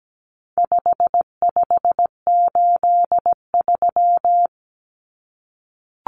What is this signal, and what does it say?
Morse code, heard as a keyed tone, about 17 wpm, 700 Hz. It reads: 5583
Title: Common Cut Numbers - 4-Digit 17wpm